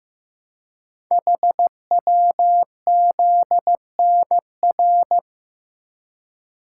Morse code, heard as a keyed tone, about 15 wpm, 700 Hz. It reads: HWZNR
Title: Code Groups - 5 Letter:Numbers - 15wpm